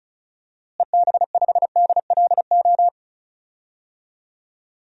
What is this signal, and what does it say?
Morse code, heard as a keyed tone, about 34 wpm, 700 Hz. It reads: EB5DLO
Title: DXCC-100 Call Signs - 6 Characters - Repeated 2x - 35wpm